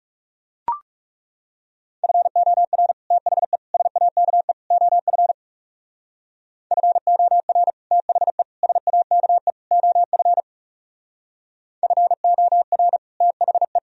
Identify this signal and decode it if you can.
Morse code, heard as a keyed tone, about 35 wpm, 700 Hz. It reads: FOR THE SAKE OF FOR THE SAKE OF FOR THE
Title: Speed-Racing - Sets of 4 Words 30wpm